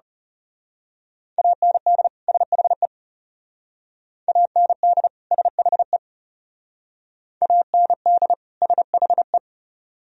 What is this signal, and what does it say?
Morse code, heard as a keyed tone, about 34 wpm, 700 Hz. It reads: AND SHE AND SHE AND SHE
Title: Speed-Racing - Sets of 2 Words 30wpm